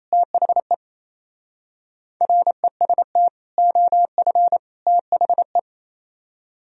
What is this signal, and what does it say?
Morse code, heard as a keyed tone, about 37 wpm, 700 Hz. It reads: THE REST OF THE